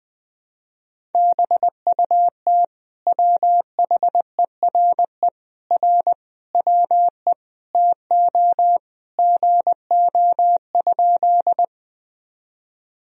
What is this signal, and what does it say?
Morse code, heard as a keyed tone, about 21 wpm, 700 Hz. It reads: BUT WHERE R WE TO GO?
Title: Sentences from Top 200 Words and Abbreviations 20wpm